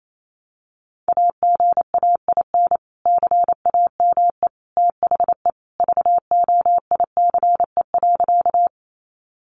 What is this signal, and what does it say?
Morse code, heard as a keyed tone, about 28 wpm, 700 Hz. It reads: AGAIN CAME THE VOICE.
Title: Sentences from Top 500 Words 28wpm